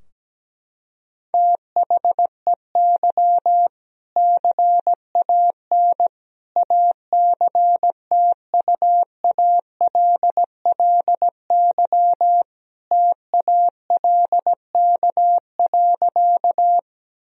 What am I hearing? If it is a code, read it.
Morse code, heard as a keyed tone, about 17 wpm, 700 Hz. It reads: THEY CAN ACTUALLY TALK.
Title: Sentences from Top 1000 Words - Review Part 4 - 17wpm